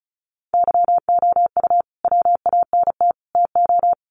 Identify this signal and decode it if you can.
Morse code, heard as a keyed tone, about 34 wpm, 700 Hz. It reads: YOU WANT TO
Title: Sentences from Top 600 Words 35wpm